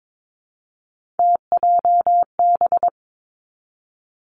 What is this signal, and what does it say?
Morse code, heard as a keyed tone, about 22 wpm, 700 Hz. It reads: TJB
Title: Three Challenging Letter-Number Combinations 22wpm